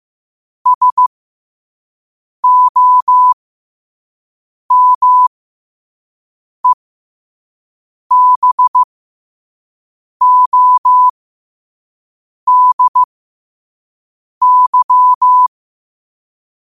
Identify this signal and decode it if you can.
Morse code, heard as a keyed tone, about 15 wpm, spaced as for 5 wpm, 1000 Hz. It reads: SOMEBODY
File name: qotdfiles / 5wpm